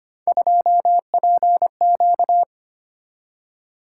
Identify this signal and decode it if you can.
Morse code, heard as a keyed tone, about 25 wpm, 700 Hz. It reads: 2PQ
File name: Three Letter-Number Combinations - 25wpm